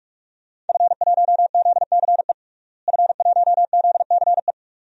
Morse code, heard as a keyed tone, about 44 wpm, 700 Hz. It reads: F1ZCE F1ZCE
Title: DXCC-100 Call Signs - 5 Characters - Repeated 2x - 45wpm